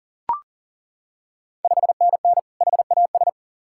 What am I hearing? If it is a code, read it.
Morse code, heard as a keyed tone, about 40 wpm, 690 Hz. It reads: 5NN HAS
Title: Ontario QSO Party Exchange 40wpm